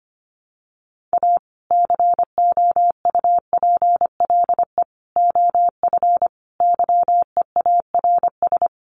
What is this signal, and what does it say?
Morse code, heard as a keyed tone, about 25 wpm, 700 Hz. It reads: A COUPLE OF YEARS
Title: Sets of 5 Words 25wpm